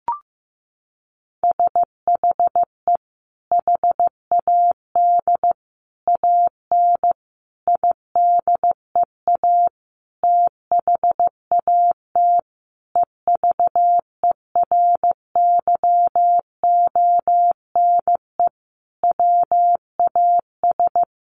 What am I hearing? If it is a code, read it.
Morse code, heard as a keyed tone, about 15 wpm, 700 Hz. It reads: SHE HAD AN IDEA THAT EVERYONE WAS
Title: Sentences from Top 1400 Words 15wpm